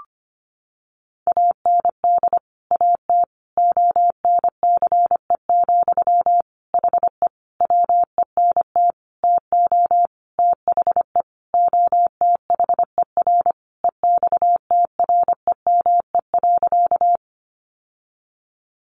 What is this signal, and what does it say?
Morse code, heard as a keyed tone, about 25 wpm, 700 Hz. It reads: AND AT ONCE, HE WENT TO THE OTHER EXTREME.
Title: Sentences from Top 1300 Words 25wpm